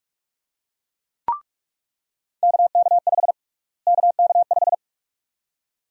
Morse code, heard as a keyed tone, about 41 wpm, 700 Hz. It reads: KK5 KK5
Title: USA Call Sign Prefixes - Repeated 2x - 45wpm